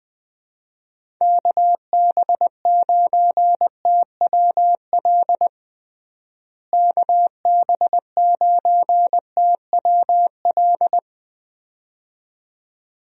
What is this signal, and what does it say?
Morse code, heard as a keyed tone, about 20 wpm, 700 Hz. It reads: KB9TWL KB9TWL
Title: USA Call Signs - 6 Characters - Repeated 2x - 20wpm